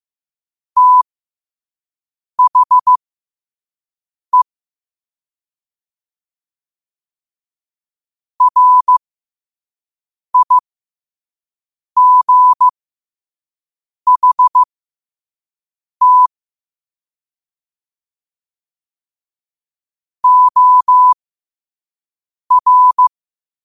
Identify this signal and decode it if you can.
Morse code, heard as a keyed tone, about 15 wpm, spaced as for 5 wpm, 1000 Hz. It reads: THE RIGHT OR
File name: qotdfiles / 5wpm